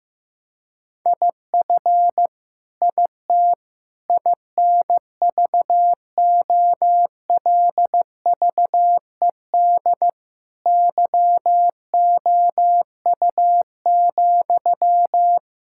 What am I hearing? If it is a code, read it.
Morse code, heard as a keyed tone, about 15 wpm, 700 Hz. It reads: IF IT INVOLVED YOU,
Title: Sentences from Top 1000 Words - Review Part 5 - 15wpm